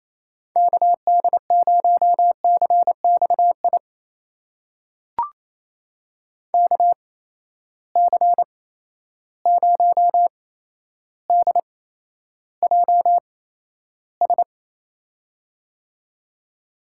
Morse code, heard as a keyed tone, about 28 wpm, spaced as for 12 wpm, 700 Hz. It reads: KD0CXI KC0DJS
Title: USA Call Signs - 6 Characters - 8x Character Spacing - 28wpm